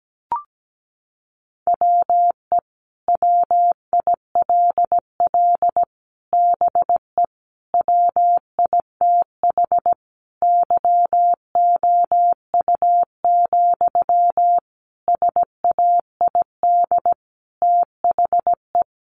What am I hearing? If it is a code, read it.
Morse code, heard as a keyed tone, about 17 wpm, 700 Hz. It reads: WE WILL BE WITH YOU, SAID THE
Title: Sentences from Top 600 Words 17wpm